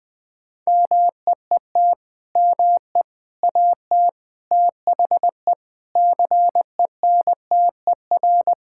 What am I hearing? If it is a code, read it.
Morse code, heard as a keyed tone, about 20 wpm, 700 Hz. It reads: MEET ME AT THE CENTER
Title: Sentences with 3-Word Phrases - 20wpm